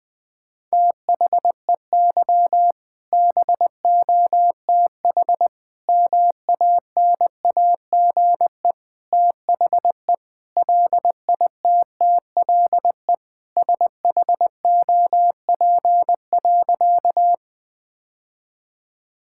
Morse code, heard as a keyed tone, about 20 wpm, 700 Hz. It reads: THEY BOTH MANAGE THE LITTLE SHOP.
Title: Sentences from Top 1600 Words 20wpm